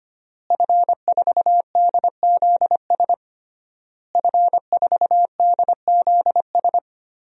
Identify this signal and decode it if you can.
Morse code, heard as a keyed tone, about 25 wpm, 700 Hz. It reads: F4DZS F4DZS
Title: DXCC-100 Call Signs - 5 Characters - Repeated 3x - 25wpm